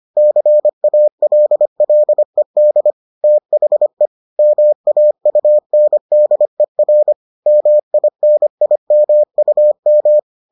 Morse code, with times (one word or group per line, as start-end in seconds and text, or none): 0.2-2.9 CALLED
3.2-4.1 THE
4.4-7.1 MAUNDER
7.5-10.2 MINIMUM